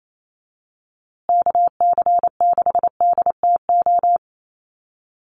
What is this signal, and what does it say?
Morse code, heard as a keyed tone, about 28 wpm, 700 Hz. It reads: KC6DTO
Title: USA Call Signs Repeated 3x 28wpm